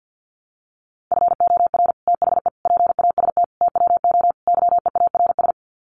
Morse code, heard as a keyed tone, about 50 wpm, 700 Hz. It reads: FOR THE PAST TWO YEARS
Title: Sets of 5 Words Encore2 50wpm